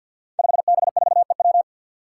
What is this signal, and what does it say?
Morse code, heard as a keyed tone, about 51 wpm, 700 Hz. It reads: 5B4EW
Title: Five Challenging Letter-Number Combinations 50wpm